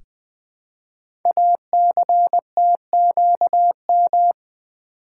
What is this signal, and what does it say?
Morse code, heard as a keyed tone, about 20 wpm, 700 Hz. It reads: ACTQM
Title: Code Groups - 5 Letters - 20wpm